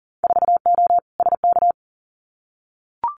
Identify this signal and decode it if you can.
Morse code, heard as a keyed tone, about 42 wpm, 700 Hz. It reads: 4O SK
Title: Winter Field Day Exchanges Encore 40wpm